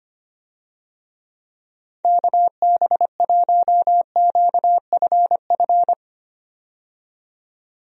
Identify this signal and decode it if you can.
Morse code, heard as a keyed tone, about 25 wpm, 700 Hz. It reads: KB1QFF
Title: USA Call Signs - 6 Characters - 25wpm